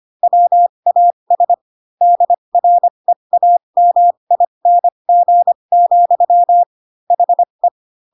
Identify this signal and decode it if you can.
Morse code, heard as a keyed tone, about 25 wpm, spaced as for 22 wpm, 700 Hz. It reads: WAS DREAMING, HE